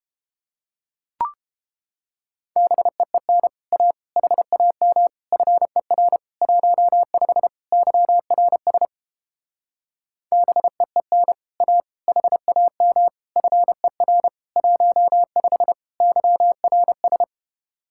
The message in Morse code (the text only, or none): BEEN A HAM FER 15 YRS BEEN A HAM FER 15 YRS